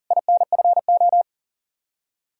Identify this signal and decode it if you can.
Morse code, heard as a keyed tone, about 50 wpm, 700 Hz. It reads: INFO
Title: Speed-Racing Common QSO Phrases - 40wpm